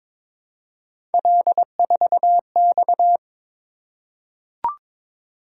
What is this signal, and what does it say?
Morse code, heard as a keyed tone, about 22 wpm, 700 Hz. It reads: L4X
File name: practice / 22wpm / Three Letter-Number Combinations - 22wpm